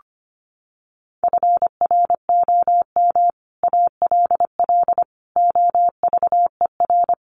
Morse code, heard as a keyed tone, about 25 wpm, 700 Hz. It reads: FROM ALL OVER
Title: Sets of 5 Words 25wpm